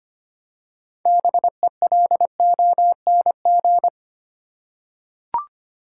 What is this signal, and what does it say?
Morse code, heard as a keyed tone, about 25 wpm, 700 Hz. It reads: BELONG
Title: Six Letter Words - 25wpm